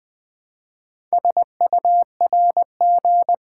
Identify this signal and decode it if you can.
Morse code, heard as a keyed tone, about 20 wpm, 700 Hz. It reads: SURG
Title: Build A Word - 2 Letters At A Time - 8-Letter Words - 20wpm